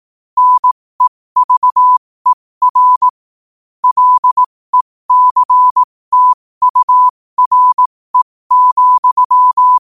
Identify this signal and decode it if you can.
Morse code, heard as a keyed tone, about 18 wpm, spaced as for 15 wpm, 1000 Hz. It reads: NEVER LECTURE,
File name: qotdfiles / 15wpm